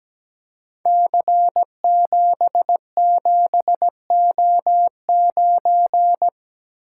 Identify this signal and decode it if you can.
Morse code, heard as a keyed tone, about 17 wpm, 700 Hz. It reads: C77O9
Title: Code Groups - 5 Letter:Numbers - 17wpm